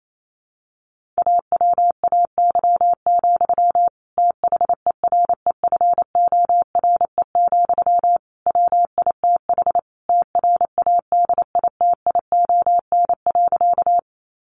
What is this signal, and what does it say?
Morse code, heard as a keyed tone, about 28 wpm, 700 Hz. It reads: AWAY, THEREFORE, WITH TRADITION.
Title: Sentences from Top 1500 Words 28wpm